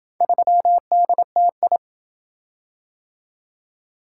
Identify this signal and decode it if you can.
Morse code, heard as a keyed tone, about 26 wpm, 700 Hz. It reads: 3DTI